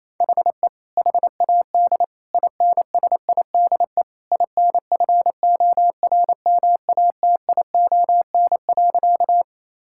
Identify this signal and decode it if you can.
Morse code, heard as a keyed tone, about 28 wpm, 700 Hz. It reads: HE HAD INSIDE INFORMATION.
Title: Sentences from Top 1000 Words - Review Part 6 - 28wpm